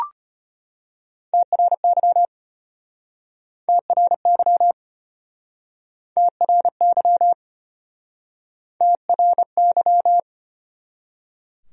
Morse code, heard as a keyed tone, about 33 wpm, 700 Hz. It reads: TRY TRY TRY TRY